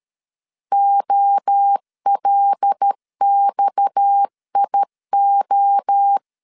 Morse code, heard as a keyed tone, about 13 wpm, 800 Hz. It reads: OLXIO
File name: LEZIONI / 8 YJ45 / 8 YJ 12wpm 2235